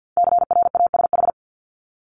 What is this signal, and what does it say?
Morse code, heard as a keyed tone, about 52 wpm, 700 Hz. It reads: CRASH
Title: Top 1751-2000 Words 50wpm